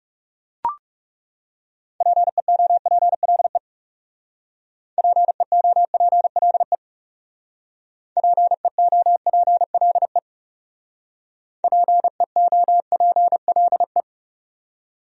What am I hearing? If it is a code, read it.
Morse code, heard as a keyed tone, about 37 wpm, 700 Hz. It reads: PEOPLE PEOPLE PEOPLE PEOPLE
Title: Speed-Racing Top 300 Words 30wpm